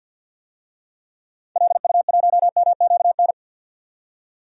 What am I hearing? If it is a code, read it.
Morse code, heard as a keyed tone, about 50 wpm, 700 Hz. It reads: LU1MQN
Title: DXCC-100 Call Signs - 6 Characters - 50wpm